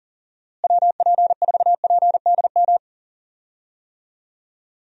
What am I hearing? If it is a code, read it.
Morse code, heard as a keyed tone, about 40 wpm, 700 Hz. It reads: WP4PDM
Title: USA Call Signs - 6 Characters - Repeated 3x - 40wpm